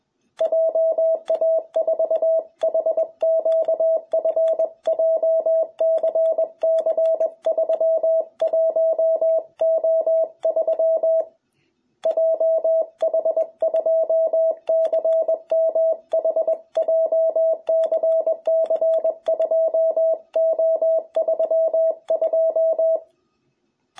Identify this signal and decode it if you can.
Morse code, heard as a keyed tone, about 20 wpm, 660 Hz. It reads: JA4HQF JCC31O3 JH2CMH JCC2O32
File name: jh8cbh / 2021cw